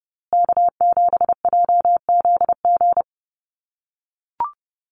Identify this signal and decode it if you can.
Morse code, heard as a keyed tone, about 30 wpm, 700 Hz. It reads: K7JZG
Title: USA Call Signs Repeated 3x 30wpm